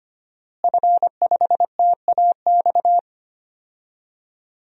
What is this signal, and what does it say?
Morse code, heard as a keyed tone, about 25 wpm, 700 Hz. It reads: F5TAX
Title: DXCC-100 Call Signs - 5 Characters - Repeated 2x - 25wpm